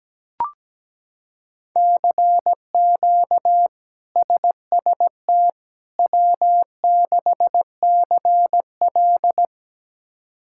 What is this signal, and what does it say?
Morse code, heard as a keyed tone, about 17 wpm, 700 Hz. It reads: CQ SST W6CL
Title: K1USN SST QSO Practice - 17wpm